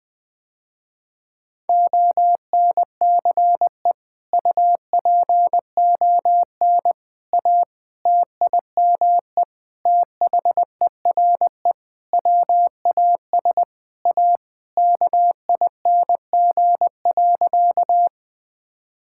Morse code, heard as a keyed tone, about 20 wpm, 700 Hz. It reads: ONCE UPON A TIME THERE WAS A KING.
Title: Sentences with 7-Word Phrases - Spoken Beginning - 20wpm